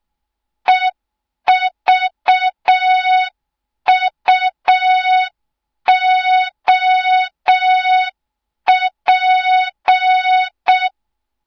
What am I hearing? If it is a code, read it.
Morse code, heard as a keyed tone, about 6 wpm, 3700 Hz. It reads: EVUOP